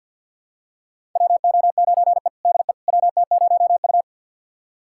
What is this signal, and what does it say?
Morse code, heard as a keyed tone, about 51 wpm, 700 Hz. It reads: WO9E DE WT0U
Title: USA Callsign DE Callsign - 4 Characters - 50wpm